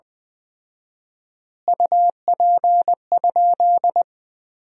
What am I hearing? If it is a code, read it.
Morse code, heard as a keyed tone, about 20 wpm, 700 Hz. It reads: UP?
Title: 4x Word-Spacing - Sentences from Top 200 Words - 20wpm